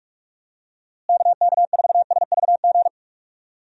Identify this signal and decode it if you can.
Morse code, heard as a keyed tone, about 44 wpm, 700 Hz. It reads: KK4SVG
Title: USA Call Signs - 6 Characters - Repeated 3x - 45wpm